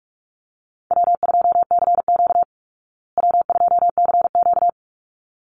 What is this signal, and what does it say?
Morse code, heard as a keyed tone, about 44 wpm, 700 Hz. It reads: W2CQ W2CQ
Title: Call Signs Repeated 3x - 45wpm